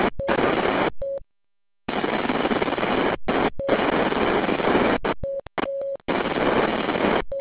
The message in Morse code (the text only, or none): none